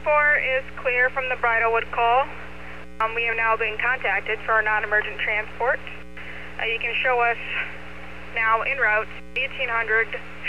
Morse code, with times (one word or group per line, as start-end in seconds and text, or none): none